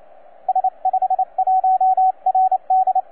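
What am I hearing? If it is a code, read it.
Morse code, heard as a keyed tone, about 28 wpm, 750 Hz, with radio noise. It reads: S51RD
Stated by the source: TZ6BB on 12 metres CW, from DX World.net.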